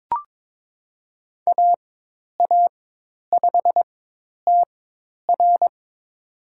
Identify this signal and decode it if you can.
Morse code, heard as a keyed tone, about 22 wpm, spaced as for 10 wpm, 700 Hz. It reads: AA5TR
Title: USA Call Signs - 5 Characters - 4x Character Spacing - 22wpm